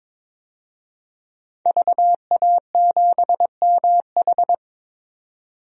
Morse code, heard as a keyed tone, about 21 wpm, 700 Hz. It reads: VA7MH
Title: DXCC-100 Call Signs - 5 Characters - 22wpm